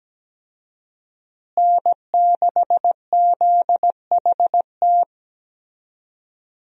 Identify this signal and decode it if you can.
Morse code, heard as a keyed tone, about 17 wpm, 700 Hz. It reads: N6ZHT
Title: USA Call Signs - 5 Characters - Repeated 3x - 17wpm